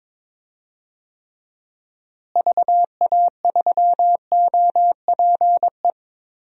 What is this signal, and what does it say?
Morse code, heard as a keyed tone, about 21 wpm, 700 Hz. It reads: VA3OPE
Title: DXCC-100 Call Signs - 6 Characters - Repeated 2x - 22wpm